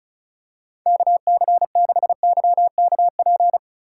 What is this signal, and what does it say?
Morse code, heard as a keyed tone, about 36 wpm, 700 Hz. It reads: KC6YKP
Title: USA Call Signs - 6 Characters - 8x Character Spacing - 35wpm